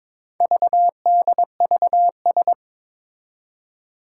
Morse code, heard as a keyed tone, about 22 wpm, 700 Hz. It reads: VDVS
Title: Four Challenging Letter-Number Combinations 22wpm